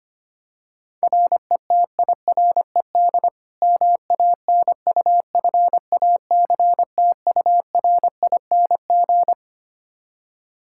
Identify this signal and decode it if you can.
Morse code, heard as a keyed tone, about 25 wpm, 700 Hz. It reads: RETIRED MANUFACTURING
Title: K1USN SST QSO Practice 25wpm